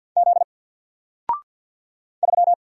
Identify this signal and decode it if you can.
Morse code, heard as a keyed tone, about 50 wpm, 700 Hz. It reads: Z 3
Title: Single Letter-Number 50wpm